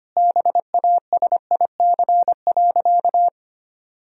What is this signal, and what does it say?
Morse code, heard as a keyed tone, about 25 wpm, 700 Hz. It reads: BASIC.
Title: Sentences from Top 1000 Words - Review Part 3 - 25wpm